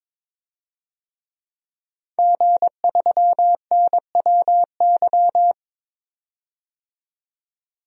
Morse code, heard as a keyed tone, about 23 wpm, 700 Hz. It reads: G3NWY
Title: DXCC-100 Call Signs - 5 Characters - 22wpm